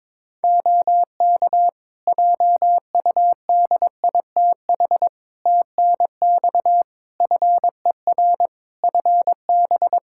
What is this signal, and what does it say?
Morse code, heard as a keyed tone, about 22 wpm, 700 Hz. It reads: OK JUDITH TNX FER FB
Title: Practice QSO - Exchange 4 - Ending - 22wpm